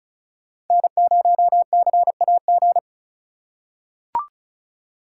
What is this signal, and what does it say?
Morse code, heard as a keyed tone, about 36 wpm, 700 Hz. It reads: N0CAG
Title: USA Call Signs - 5 Characters - 2x Character Spacing - 35wpm